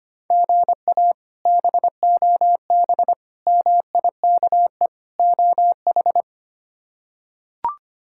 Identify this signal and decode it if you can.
Morse code, heard as a keyed tone, about 25 wpm, 700 Hz. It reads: GA BOB MIKE OH
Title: K1USN SST QSO Practice - 25wpm